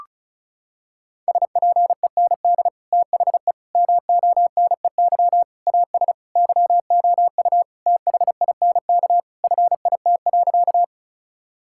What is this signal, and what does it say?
Morse code, heard as a keyed tone, about 34 wpm, 700 Hz. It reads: SPEND THE MONEY AS YOU THINK FIT.
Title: Sentences from Top 1000 Words - Review Part 5 - 35wpm